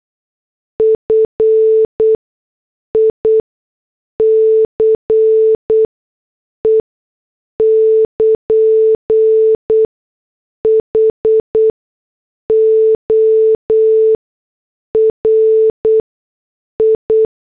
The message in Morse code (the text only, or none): FICE <KN> HORI